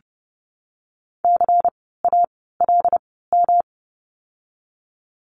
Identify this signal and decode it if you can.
Morse code, heard as a keyed tone, about 30 wpm, spaced as for 17 wpm, 700 Hz. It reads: CALM
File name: course / 30:10wpm / 108 Lesson 18f - Letter m - Words 30wpm